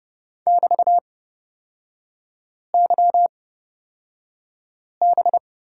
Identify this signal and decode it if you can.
Morse code, heard as a keyed tone, about 30 wpm, spaced as for 5 wpm, 700 Hz. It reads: <BT>YB
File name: Lesson 36e - BT - Three Characters 30wpm